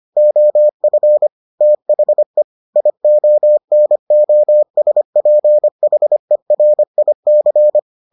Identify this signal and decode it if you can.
Morse code, heard as a keyed tone, about 25 wpm, 600 Hz. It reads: OF THE IONOSPHERIC